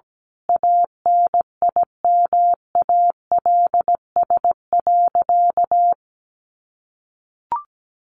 Morse code, heard as a keyed tone, about 17 wpm, 700 Hz. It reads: ANIMALS.
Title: A Sound Of Thunder 17wpm